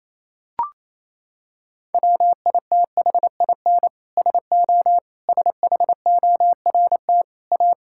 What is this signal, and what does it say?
Morse code, heard as a keyed tone, about 28 wpm, 700 Hz. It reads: WITHIN SO SHORT A
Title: Sentences from Top 1000 Words - Review Part 5 - 28wpm